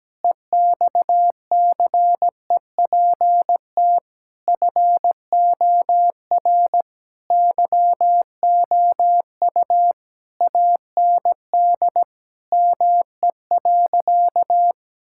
Morse code, heard as a keyed tone, about 17 wpm, 700 Hz. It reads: EXCEPT FOR YOU AND ME.
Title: Sentences from Top 1600 Words 17wpm